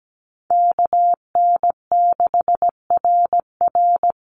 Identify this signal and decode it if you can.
Morse code, heard as a keyed tone, about 17 wpm, 700 Hz. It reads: KN6RR
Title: USA Call Signs 17wpm